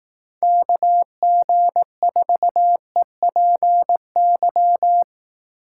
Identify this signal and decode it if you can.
Morse code, heard as a keyed tone, about 18 wpm, 700 Hz. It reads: KG4EPY
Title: Speed Racing - USA 2x3 Call Signs 15wpm